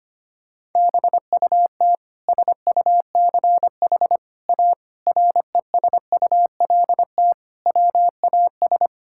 Morse code, heard as a keyed tone, about 25 wpm, 700 Hz. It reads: BUT SUCH A RESULT WAS
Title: Sentences from Top 1000 Words - Review Part 5 - 25wpm